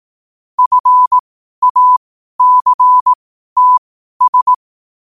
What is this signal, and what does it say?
Morse code, heard as a keyed tone, about 18 wpm, spaced as for 12 wpm, 1000 Hz. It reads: FACTS
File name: qotdfiles / 12wpm